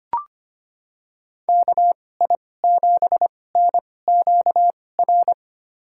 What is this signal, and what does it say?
Morse code, heard as a keyed tone, about 25 wpm, spaced as for 18 wpm, 700 Hz. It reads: KI7NQR
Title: USA Call Signs - 6 Characters - 2x Character Spacing - 25wpm